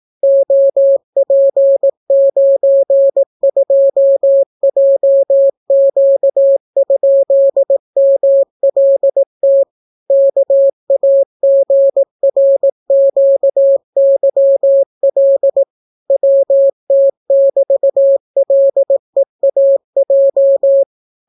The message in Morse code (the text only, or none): OP92JQ?MLT KAGRQYL WT<BT>LEAJ